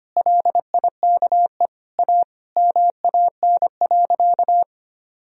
Morse code, heard as a keyed tone, about 25 wpm, 700 Hz. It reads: LIKE A MAN.
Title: Sentences from Top 300 Words and Abbreviations 25wpm